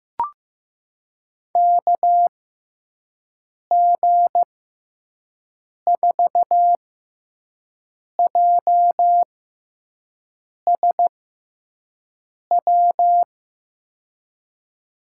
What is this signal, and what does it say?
Morse code, heard as a keyed tone, about 15 wpm, spaced as for 5 wpm, 700 Hz. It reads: KG4JSW
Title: USA Call Signs - 6 Characters - 6x Character Spacing - 15wpm